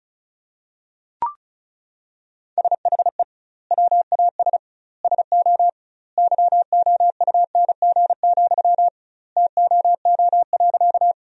Morse code, heard as a keyed tone, about 36 wpm, 700 Hz. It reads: SHE WAS SO YOUNG, TOO.
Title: 2x Word-Spacing - Sentences from Top 400 Words - 35wpm